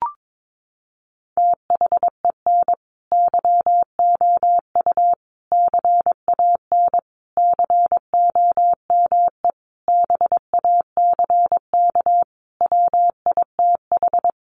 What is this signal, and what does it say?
Morse code, heard as a keyed tone, about 22 wpm, 700 Hz. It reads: THEN YOU CAN COME BACK WITH
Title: A Sound Of Thunder 22wpm